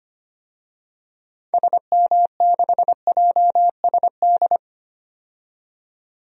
Morse code, heard as a keyed tone, about 25 wpm, 700 Hz. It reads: SM6JSD
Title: DXCC-100 Call Signs - 6 Characters - 25wpm